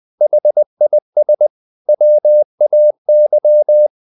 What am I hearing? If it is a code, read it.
Morse code, heard as a keyed tone, about 20 wpm, 600 Hz. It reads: HIS WAY
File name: the old man and the sea part 2 20wpm